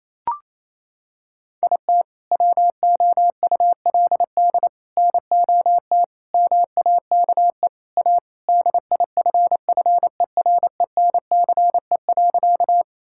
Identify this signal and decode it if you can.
Morse code, heard as a keyed tone, about 28 wpm, 700 Hz. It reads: IT WOULD NOT MAKE A DIFFERENCE.
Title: Sentences from Top 1000 Words - Review Part 5 - 28wpm